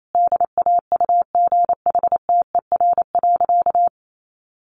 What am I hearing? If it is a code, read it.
Morse code, heard as a keyed tone, about 28 wpm, 700 Hz. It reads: DAUGHTER.
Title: Sentences from Top 1500 Words 28wpm